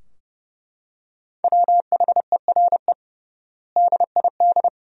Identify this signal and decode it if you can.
Morse code, heard as a keyed tone, about 30 wpm, 700 Hz. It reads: WHERE DID
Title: 3x Word-Spacing - Sentences from Top 200 Words - 30wpm